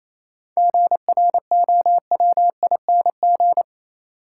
Morse code, heard as a keyed tone, about 28 wpm, 700 Hz. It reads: GROWING
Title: Sets of 5 Words Encore - New Word Drill - 28wpm